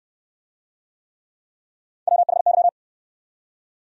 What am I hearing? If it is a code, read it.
Morse code, heard as a keyed tone, about 69 wpm, 700 Hz. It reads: WHY